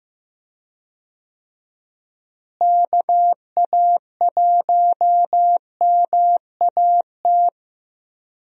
Morse code, heard as a keyed tone, about 15 wpm, 700 Hz. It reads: KA1MAT
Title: USA Call Signs - 6 Characters - Repeated 3x - 15wpm